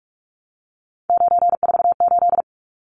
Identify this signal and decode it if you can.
Morse code, heard as a keyed tone, about 44 wpm, 700 Hz. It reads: 948
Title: Numbers - 3-Digit 45wpm